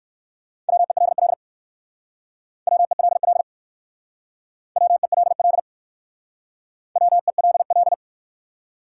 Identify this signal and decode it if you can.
Morse code, heard as a keyed tone, about 60 wpm, 700 Hz. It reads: WELL WELL WELL WELL